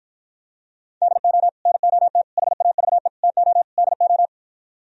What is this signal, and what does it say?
Morse code, heard as a keyed tone, about 52 wpm, 700 Hz. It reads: DO NOT HAVE TO DO